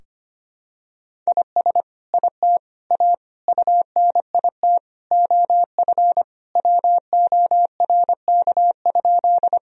Morse code, heard as a keyed tone, about 25 wpm, 700 Hz. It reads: IS IT A UNIT OF WORK?
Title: Sentences from Top 1000 Words - Review Part 2 - 25wpm